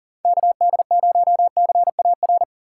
Morse code, heard as a keyed tone, about 37 wpm, 700 Hz. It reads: KD0CAR
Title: USA Call Signs - 6 Characters - 2x Character Spacing - 40wpm